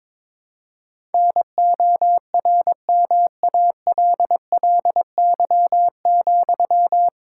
Morse code, heard as a keyed tone, about 22 wpm, 700 Hz. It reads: NORMALLY,
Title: Sentences from Top 1000 Words - Review Part 6 - 22wpm